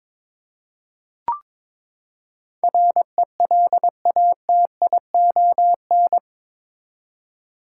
Morse code, heard as a keyed tone, about 22 wpm, 700 Hz. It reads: RELATION
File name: practice / 22wpm / Eight Letter Words - 22wpm